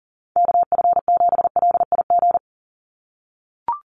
Morse code, heard as a keyed tone, about 40 wpm, 700 Hz. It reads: KF7LIG
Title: USA Call Signs 40wpm